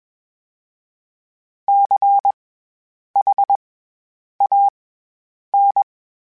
Morse code, heard as a keyed tone, about 21 wpm, spaced as for 8 wpm, 800 Hz. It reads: CHAN
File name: Special INGs-mp3s / Special INGs-mp3s / SlowToFaster ing @20